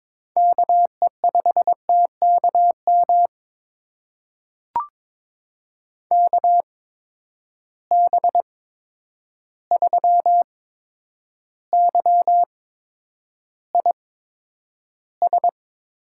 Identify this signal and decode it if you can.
Morse code, heard as a keyed tone, about 22 wpm, spaced as for 9 wpm, 700 Hz. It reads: KE5TKM KB3YIS
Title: USA Call Signs - 6 Characters - 8x Character Spacing - 22wpm